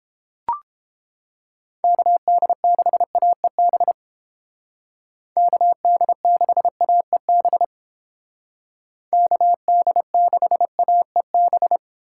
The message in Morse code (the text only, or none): KD6AEB KD6AEB KD6AEB